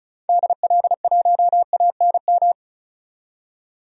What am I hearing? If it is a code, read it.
Morse code, heard as a keyed tone, about 36 wpm, 700 Hz. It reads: DL1ANM
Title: DXCC-100 Call Signs - 6 Characters - 35wpm